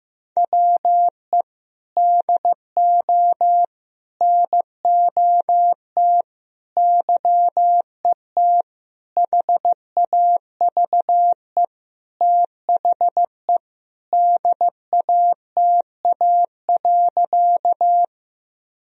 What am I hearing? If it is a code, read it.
Morse code, heard as a keyed tone, about 15 wpm, 700 Hz. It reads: WE DO NOT YET HAVE THE DATA.
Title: Sentences from Top 1000 Words - Review Part 3 - 15wpm